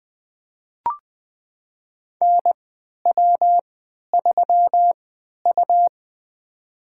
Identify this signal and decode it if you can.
Morse code, heard as a keyed tone, about 20 wpm, spaced as for 11 wpm, 700 Hz. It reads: NW3U
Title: USA Call Signs - 4 Characters - 3x Character Spacing - 20wpm